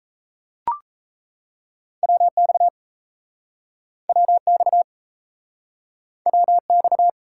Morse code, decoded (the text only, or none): WX WX WX